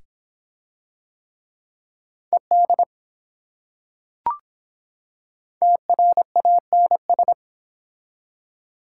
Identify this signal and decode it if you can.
Morse code, heard as a keyed tone, about 24 wpm, 700 Hz. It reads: ED TRANS